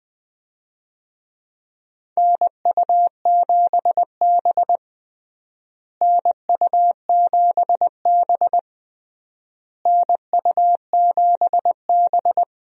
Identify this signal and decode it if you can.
Morse code, heard as a keyed tone, about 21 wpm, 700 Hz. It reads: NU7B NU7B NU7B
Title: USA Call Signs - 4 Characters - Repeated 3x - 20wpm